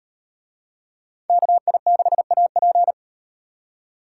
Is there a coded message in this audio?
Yes: Morse code from 1.3 to 2.9 s, heard as a keyed tone, about 38 wpm, 700 Hz. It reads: KI6AP